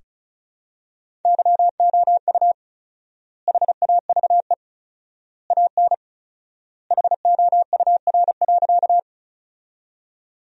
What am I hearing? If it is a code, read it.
Morse code, heard as a keyed tone, about 36 wpm, 700 Hz. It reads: YOU HAVE AN HOUR.
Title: 4x Word-Spacing - Sentences from Top 400 Words - 35wpm